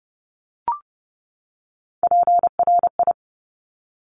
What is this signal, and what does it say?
Morse code, heard as a keyed tone, about 30 wpm, 690 Hz. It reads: PRI
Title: Top Three Letter Combinations in Words 30wpm